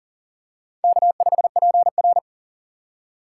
Morse code, heard as a keyed tone, about 40 wpm, 700 Hz. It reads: K5PR
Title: USA Call Signs - 4 Characters - 6x Character Spacing - 40wpm